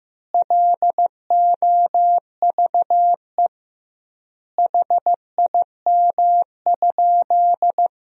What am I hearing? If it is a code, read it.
Morse code, heard as a keyed tone, about 16 wpm, 700 Hz. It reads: LOVE HIM?
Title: 2x Word-Spacing - Sentences from Top 400 Words - 15wpm